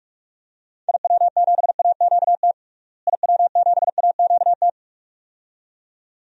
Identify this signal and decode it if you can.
Morse code, heard as a keyed tone, about 46 wpm, 700 Hz. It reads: IW7AQT IW7AQT
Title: DXCC-100 Call Signs - 6 Characters - Repeated 2x - 45wpm